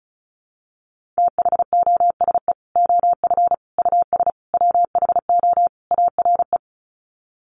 Morse code, heard as a keyed tone, about 34 wpm, 700 Hz. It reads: THOSE OF US WHO ARE
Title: Sets of 5 Words Encore2 35wpm